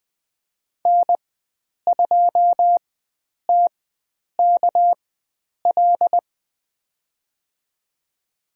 Morse code, heard as a keyed tone, about 20 wpm, spaced as for 9 wpm, 700 Hz. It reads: N2TKL
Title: USA Call Signs - 5 Characters - 4x Character Spacing - 20wpm